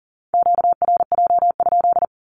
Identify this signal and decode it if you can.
Morse code, heard as a keyed tone, about 38 wpm, 700 Hz. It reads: QRJ?
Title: Q-Codes with Spoken Meaning - 40wpm